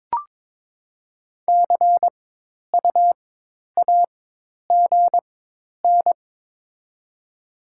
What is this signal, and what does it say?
Morse code, heard as a keyed tone, about 22 wpm, spaced as for 10 wpm, 700 Hz. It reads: CUAGN
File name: Top 20 QSO Elements - 4x Character Spacing - 22wpm